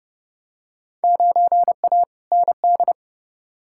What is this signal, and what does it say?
Morse code, heard as a keyed tone, about 30 wpm, 700 Hz. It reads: 9A ND
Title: Field Day Exchanges Encore - 30wpm